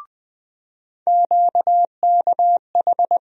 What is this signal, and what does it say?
Morse code, heard as a keyed tone, about 20 wpm, 700 Hz. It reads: QKH
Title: USA Call Sign Postfixes - 3 Letters 20wpm